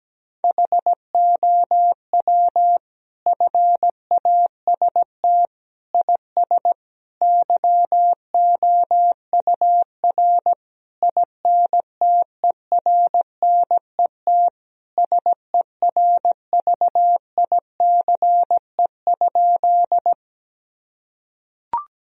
Morse code, heard as a keyed tone, about 17 wpm, 700 Hz. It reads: HOW FAST IS YOUR INTERNET SERVICE?
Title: Sentences from Top 1000 Words - Review Part 4 - 17wpm